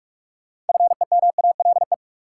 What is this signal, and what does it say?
Morse code, heard as a keyed tone, about 45 wpm, 700 Hz. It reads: FEMALE
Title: Six Letter Words - 45wpm